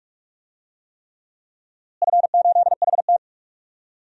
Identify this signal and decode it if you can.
Morse code, heard as a keyed tone, about 43 wpm, 700 Hz. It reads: F8HT
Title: DXCC-100 Call Signs - 4 Characters - Repeated 3x - 45wpm